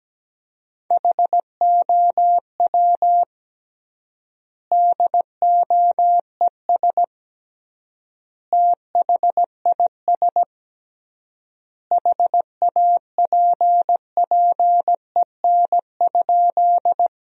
Morse code, heard as a keyed tone, about 17 wpm, 700 Hz. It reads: HOW DOES THIS HAPPEN?
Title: 3x Word-Spacing - Sentences from Top 400 Words - 17wpm